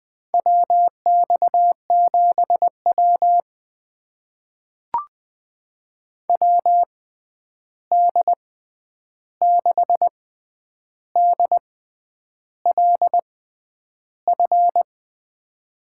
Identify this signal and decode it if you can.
Morse code, heard as a keyed tone, about 21 wpm, spaced as for 9 wpm, 700 Hz. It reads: WX7W WD6DLF
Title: USA Call Signs - 6x Character Spacing - 20wpm